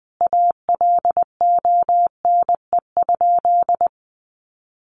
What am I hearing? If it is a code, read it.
Morse code, heard as a keyed tone, about 20 wpm, 700 Hz. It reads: ALONE?
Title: Sentences from Top 1000 Words 20wpm